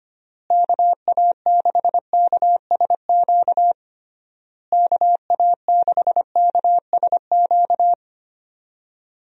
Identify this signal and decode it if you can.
Morse code, heard as a keyed tone, about 25 wpm, 700 Hz. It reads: KA6KSQ KA6KSQ
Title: USA Call Signs - 6 Characters - Repeated 3x - 25wpm